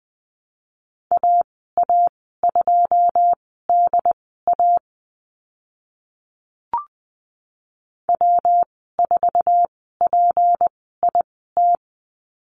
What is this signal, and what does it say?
Morse code, heard as a keyed tone, about 20 wpm, spaced as for 14 wpm, 700 Hz. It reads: AA2DA W4PIT
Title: 154 Lesson 25g - Letter p - Call Signs 20wpm